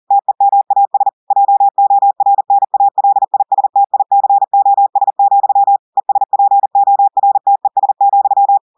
Morse code, heard as a keyed tone, about 40 wpm, 800 Hz. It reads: TEMAS JORNALISTICOS, ESPORTES,
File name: audios / portugais / portugais texte en clair 2 vitesse 40 mots minutes